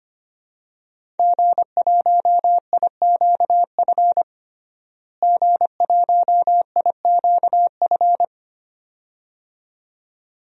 Morse code, heard as a keyed tone, about 25 wpm, 700 Hz. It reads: G1IQF G1IQF
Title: DXCC-100 Call Signs - 5 Characters - Repeated 2x - 25wpm